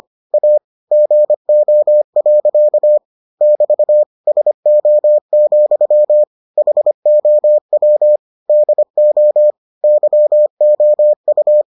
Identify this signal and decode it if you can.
Morse code, heard as a keyed tone, about 25 wpm, 600 Hz. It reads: A GO. <BT> SO, HOW DO YOU